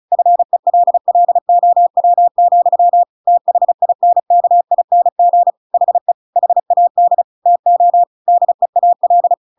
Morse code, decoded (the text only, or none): FELLOW, THINKING HE HAD TO DEAL